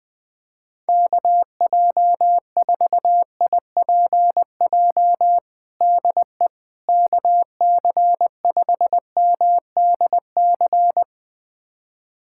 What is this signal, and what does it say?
Morse code, heard as a keyed tone, about 20 wpm, 700 Hz. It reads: KJ4IPJ DE KC5MDC
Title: USA Callsign DE Callsign - 6 Characters - 20wpm